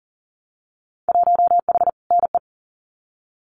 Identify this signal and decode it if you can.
Morse code, heard as a keyed tone, about 41 wpm, 700 Hz. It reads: 1H NE
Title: Winter Field Day Exchanges Encore 40wpm